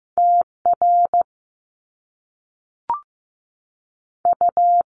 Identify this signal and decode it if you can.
Morse code, heard as a keyed tone, about 15 wpm, 700 Hz. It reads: T R U
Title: Top Two Letter Combinations in Words 15wpm